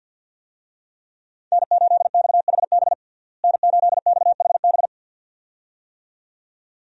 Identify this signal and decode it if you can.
Morse code, heard as a keyed tone, about 51 wpm, 700 Hz. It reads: N8XHB N8XHB
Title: USA Call Signs - 5 Characters - Repeated 2x - 50wpm